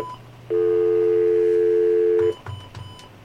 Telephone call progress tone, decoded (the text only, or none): dial tone